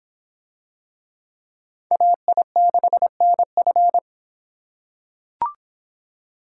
Morse code, heard as a keyed tone, about 26 wpm, 700 Hz. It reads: AI6NF